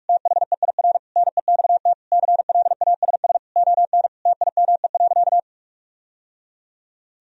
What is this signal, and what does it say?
Morse code, heard as a keyed tone, about 41 wpm, 700 Hz. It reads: THEIR NEXT CLASS ON TIME.
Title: Sentences from Top 1000 Words - Review Part 2 - 45wpm